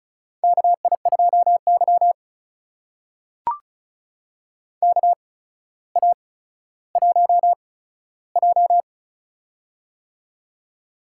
Morse code, heard as a keyed tone, about 36 wpm, spaced as for 15 wpm, 700 Hz. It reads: KI2Y KA1J
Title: USA Call Signs - 4 Characters - 8x Character Spacing - 35wpm